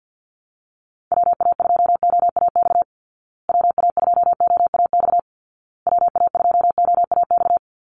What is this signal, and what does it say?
Morse code, heard as a keyed tone, about 50 wpm, 700 Hz. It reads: WA2OAX WA2OAX WA2OAX
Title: Call Signs Repeated 2x - 50wpm